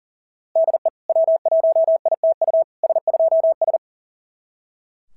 Morse code, heard as a keyed tone, about 40 wpm, 690 Hz. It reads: DE W1ITU S2S
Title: SOTA QSO Practice 40wpm